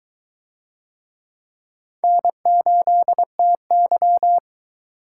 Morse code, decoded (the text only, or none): N8TY